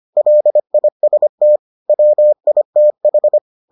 Morse code, heard as a keyed tone, about 25 wpm, 600 Hz. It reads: LIST WITH